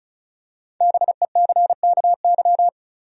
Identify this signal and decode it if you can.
Morse code, heard as a keyed tone, about 36 wpm, 700 Hz. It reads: BECKY
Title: USA Common Names - Max 5 Letters - 35wpm